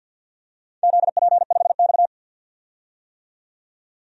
Morse code, heard as a keyed tone, about 51 wpm, 700 Hz. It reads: ZP5X
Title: DXCC-100 Call Signs - 4 Characters - 50wpm